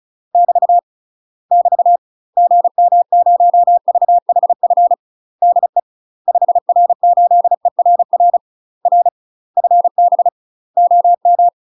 Morse code, heard as a keyed tone, about 33 wpm, 700 Hz. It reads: <BT> <BT> GM0VHF DE 5R8ERR R FB OM
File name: UK MASTER QSOs TRAINING / 35WPM / UK MASTER QSOs 0014